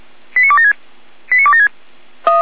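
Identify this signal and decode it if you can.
Five-tone selective call (ZVEI), heard as quick runs of five tones, tones of about 70 ms. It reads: 89276 89276
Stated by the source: ZVEI five-tone selective call received on an SDR.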